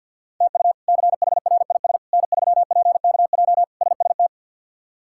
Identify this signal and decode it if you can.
Morse code, heard as a keyed tone, about 50 wpm, 700 Hz. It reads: TU CHRIS N3PKJ SST
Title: K1USN SST QSO Practice - 50wpm